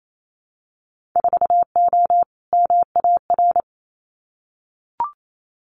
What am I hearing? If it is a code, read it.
Morse code, heard as a keyed tone, about 27 wpm, 700 Hz. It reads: 4O MAR
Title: Winter Field Day Exchanges Encore 28wpm